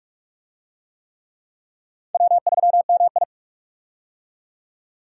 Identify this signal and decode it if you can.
Morse code, heard as a keyed tone, about 46 wpm, 700 Hz. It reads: W3MI
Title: USA Call Signs - 4 Characters - 45wpm